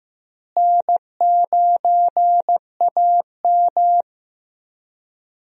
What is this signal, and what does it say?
Morse code, heard as a keyed tone, about 16 wpm, 700 Hz. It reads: N9AM
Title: USA Call Signs - 4 Characters - Repeated 2x - 15wpm